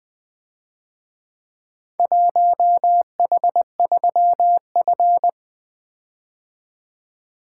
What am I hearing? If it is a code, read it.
Morse code, heard as a keyed tone, about 24 wpm, 700 Hz. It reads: 1H3F
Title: Speed-Racing - Four Letter-Number Combinations - 20wpm